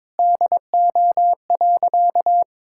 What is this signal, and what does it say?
Morse code, heard as a keyed tone, about 22 wpm, 700 Hz. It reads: DO.
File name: practice / 22wpm / Sentences with 3-Word Phrases - 22wpm